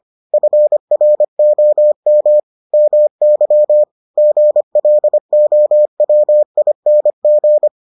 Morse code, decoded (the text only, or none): FROM MY GLOWING